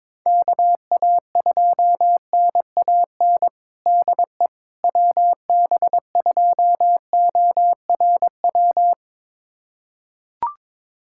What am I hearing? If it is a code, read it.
Morse code, heard as a keyed tone, about 23 wpm, 700 Hz. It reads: KA2NAN DE WB2ORW
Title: USA Callsign DE Callsign 22wpm